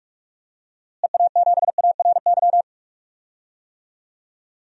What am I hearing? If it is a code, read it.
Morse code, heard as a keyed tone, about 46 wpm, 700 Hz. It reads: EA7ARY
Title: DXCC-100 Call Signs - 6 Characters - Repeated 3x - 45wpm